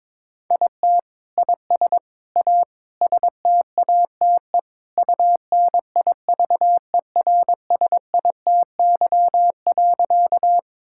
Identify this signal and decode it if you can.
Morse code, heard as a keyed tone, about 22 wpm, 700 Hz. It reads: IT IS A STATE UNIVERSITY.
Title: Sentences from Top 1000 Words - Review Part 5 - 22wpm